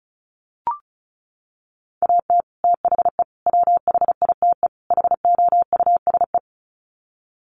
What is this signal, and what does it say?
Morse code, heard as a keyed tone, about 34 wpm, 700 Hz. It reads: AT THE WHITE HOUSE
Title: Sets of 4 Words 35wpm